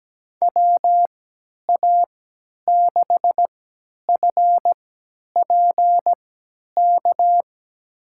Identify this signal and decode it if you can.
Morse code, heard as a keyed tone, about 17 wpm, spaced as for 10 wpm, 700 Hz. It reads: WA6FPK
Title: USA Call Signs - 6 Characters - 3x Character Spacing - 17wpm